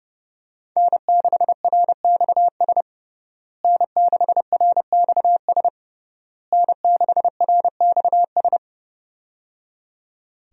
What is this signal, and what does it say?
Morse code, heard as a keyed tone, about 30 wpm, 700 Hz. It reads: N6RXS N6RXS N6RXS
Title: USA Call Signs - 5 Characters - Repeated 3x - 30wpm